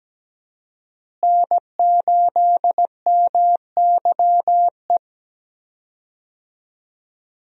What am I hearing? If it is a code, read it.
Morse code, heard as a keyed tone, about 17 wpm, 700 Hz. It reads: N8MYE
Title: USA Call Signs - 5 Characters - Repeated 2x - 17wpm